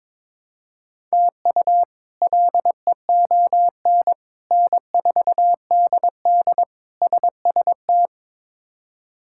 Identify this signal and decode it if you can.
Morse code, heard as a keyed tone, about 22 wpm, 700 Hz. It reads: TU LEON N4DD SST
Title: K1USN SST QSO Practice - 22wpm